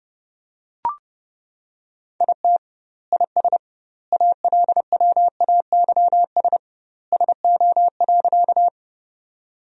Morse code, heard as a keyed tone, about 30 wpm, 700 Hz. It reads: IT IS ALWAYS SO.
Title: 2x Word-Spacing - Sentences from Top 300 Words - 30wpm